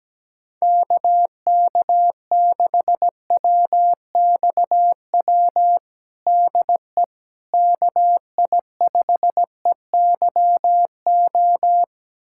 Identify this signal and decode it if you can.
Morse code, heard as a keyed tone, about 17 wpm, 700 Hz. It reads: KK6WXW DE KI5EYO
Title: USA Callsign DE Callsign - 6 Characters - 17wpm